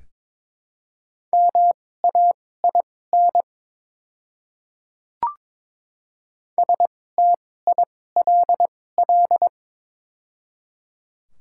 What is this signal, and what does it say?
Morse code, heard as a keyed tone, about 23 wpm, spaced as for 16 wpm, 690 Hz. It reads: MAIN STILL
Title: 2x Character Spacing - Top 300 Words - 22wpm